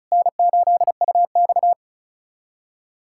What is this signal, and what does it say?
Morse code, heard as a keyed tone, about 34 wpm, 700 Hz. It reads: N8UX
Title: USA Callsign DE Callsign - 4 Characters - 35wpm